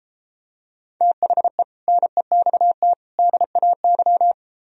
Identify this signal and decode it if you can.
Morse code, heard as a keyed tone, about 31 wpm, 700 Hz. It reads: THE NEXT DAY